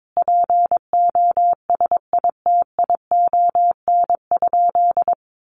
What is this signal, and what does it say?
Morse code, heard as a keyed tone, about 22 wpm, 700 Hz. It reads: POSITION?
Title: Sentences from Top 1400 Words 22wpm